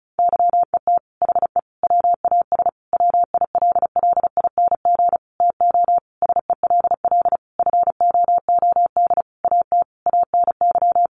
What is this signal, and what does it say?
Morse code, heard as a keyed tone, about 34 wpm, 700 Hz. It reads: YET HE WAS WILLING TO SELL FOOD AT ANY
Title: Sentences from Top 1100 Words 35wpm